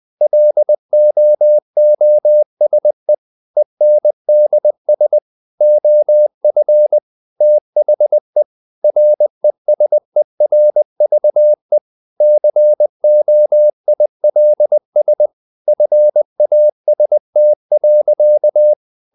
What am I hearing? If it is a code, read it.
Morse code, heard as a keyed tone, about 20 wpm, 600 Hz. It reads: LOOSE ENDS OF THE RESERVE COILS FAST.
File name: the old man and the sea / the old man and the sea part 1 20wpm